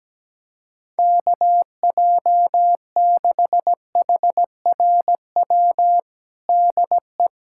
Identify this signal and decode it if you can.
Morse code, heard as a keyed tone, about 17 wpm, 700 Hz. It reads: KJ6HRW DE
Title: USA Callsign DE Callsign 17wpm